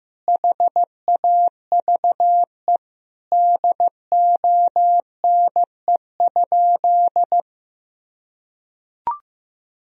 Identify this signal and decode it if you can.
Morse code, heard as a keyed tone, about 15 wpm, 700 Hz. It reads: HAVE DONE?
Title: Sentences from Top 1000 Words - Review Part 3 - 15wpm